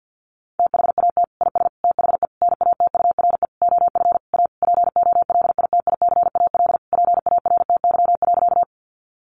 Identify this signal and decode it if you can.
Morse code, heard as a keyed tone, about 50 wpm, 700 Hz. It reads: THAT IS THE NATURE OF A POLITICAL PARTY.
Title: Sentences from Top 1400 Words 50wpm